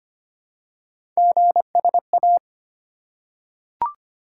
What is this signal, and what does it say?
Morse code, heard as a keyed tone, about 25 wpm, 700 Hz. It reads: GSA
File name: Three Letter-Number Combinations - 25wpm